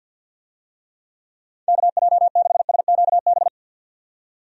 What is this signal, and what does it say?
Morse code, heard as a keyed tone, about 52 wpm, 700 Hz. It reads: KJ6SQB
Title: USA Call Signs - 6 Characters - 50wpm